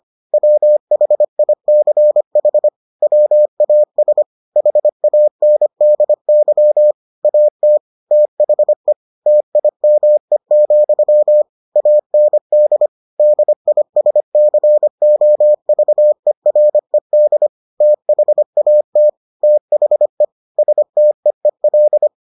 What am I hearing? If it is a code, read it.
Morse code, heard as a keyed tone, about 25 wpm, 600 Hz. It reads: WHICH WAS HANDY AT THE TIME, AND DISCOVERED THAT THE STEEL